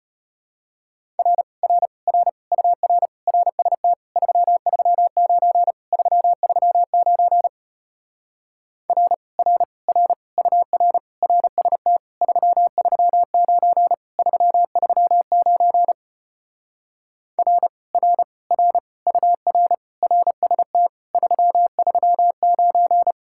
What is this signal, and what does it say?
Morse code, heard as a keyed tone, about 33 wpm, 700 Hz. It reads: R R R UR RST 339 339 R R R UR RST 339 339 R R R UR RST 339